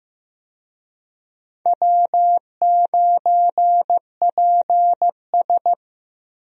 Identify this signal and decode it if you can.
Morse code, heard as a keyed tone, about 15 wpm, 700 Hz. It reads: W9PS
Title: USA Call Signs - 4 Characters - Repeated 3x - 15wpm